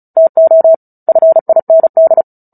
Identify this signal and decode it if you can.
Morse code, heard as a keyed tone, about 34 wpm, 650 Hz. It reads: TO FIND
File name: Book Two / 09-Chapter 9 - 35 WPM